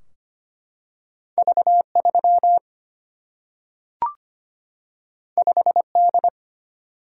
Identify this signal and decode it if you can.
Morse code, heard as a keyed tone, about 25 wpm, 700 Hz. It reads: V 3 5 D
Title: Single-Letter Single-Number 25wpm